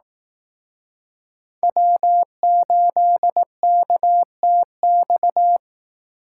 Morse code, heard as a keyed tone, about 18 wpm, 700 Hz. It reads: W8KTX
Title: Speed Racing - USA 1x3 Call Signs 15wpm